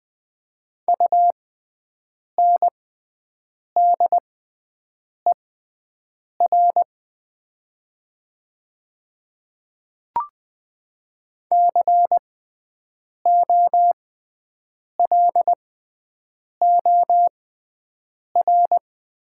Morse code, heard as a keyed tone, about 20 wpm, spaced as for 7 wpm, 700 Hz. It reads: UNDER COLOR
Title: 6x Character Spacing - Top 300 Words - 20wpm